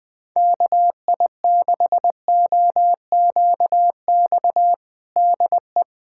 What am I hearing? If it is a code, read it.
Morse code, heard as a keyed tone, about 21 wpm, 700 Hz. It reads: KI6OQX DE
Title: USA Callsign DE Callsign 20wpm